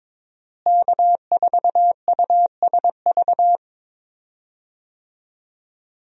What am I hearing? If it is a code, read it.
Morse code, heard as a keyed tone, about 22 wpm, 700 Hz. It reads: K4USV
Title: USA Call Signs - 5 Characters - Repeated 2x - 22wpm